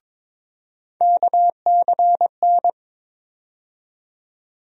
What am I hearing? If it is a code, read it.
Morse code, heard as a keyed tone, about 22 wpm, 700 Hz. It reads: KCN
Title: USA Call Sign Postfixes - 3 Letters 22wpm